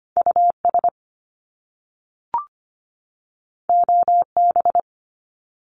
Morse code, heard as a keyed tone, about 25 wpm, 700 Hz. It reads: US OB
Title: Top Two Letter Combinations in Words 25wpm